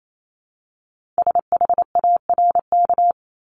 Morse code, heard as a keyed tone, about 28 wpm, 700 Hz. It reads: SHARK
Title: Multi-letter Phonemes Repeated 3x 28wpm